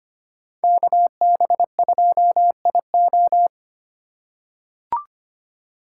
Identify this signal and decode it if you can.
Morse code, heard as a keyed tone, about 25 wpm, 700 Hz. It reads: KB2IO
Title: USA Call Signs - 5 Characters - 4x Character Spacing - 25wpm